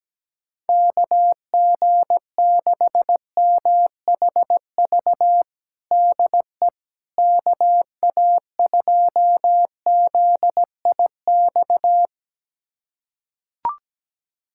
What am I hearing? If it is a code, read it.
Morse code, heard as a keyed tone, about 17 wpm, 700 Hz. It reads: KG6MHV DE KA2ZIX
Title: USA Callsign DE Callsign 17wpm